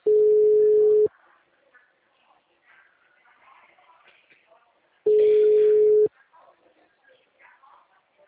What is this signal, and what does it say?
Ringback tone (Europe, 425 Hz).
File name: out- -2006